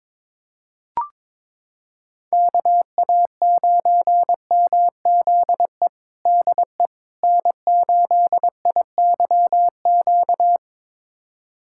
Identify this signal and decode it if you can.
Morse code, heard as a keyed tone, about 21 wpm, 700 Hz. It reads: KA9MZE DE N8IYQ
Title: USA Callsign DE Callsign 22wpm